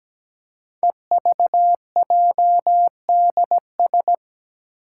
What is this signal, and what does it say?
Morse code, heard as a keyed tone, about 17 wpm, 700 Hz. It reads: EVJDS
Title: Code Groups - 5 Letters - 17wpm